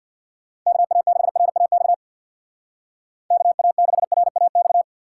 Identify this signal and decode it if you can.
Morse code, heard as a keyed tone, about 62 wpm, 700 Hz. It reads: KA6RAX KA6RAX
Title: Speed Racing - USA 2x3 Call Signs 50wpm